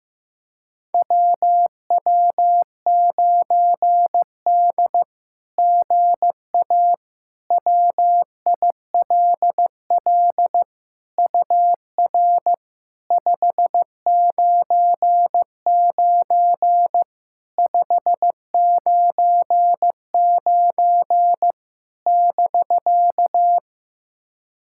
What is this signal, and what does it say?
Morse code, heard as a keyed tone, about 15 wpm, 700 Hz. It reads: WW9D GA WILL UR 599 599 <BK>
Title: SOTA QSO Practice 15wpm